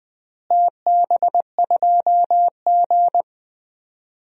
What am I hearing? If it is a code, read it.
Morse code, heard as a keyed tone, about 21 wpm, 700 Hz. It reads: TB2G
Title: DXCC-100 Call Signs - 4 Characters - Repeated 2x - 20wpm